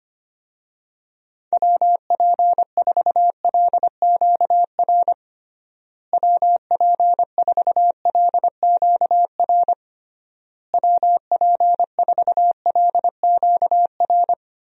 Morse code, heard as a keyed tone, about 25 wpm, 700 Hz. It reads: WP4LQR WP4LQR WP4LQR
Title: USA Call Signs - 6 Characters - Repeated 3x - 25wpm